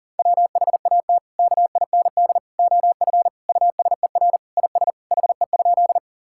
Morse code, heard as a keyed tone, about 40 wpm, 700 Hz. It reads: WHAT KIND OF USER IS HE?
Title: Sentences from Top 1000 Words - Review Part 5 - 40wpm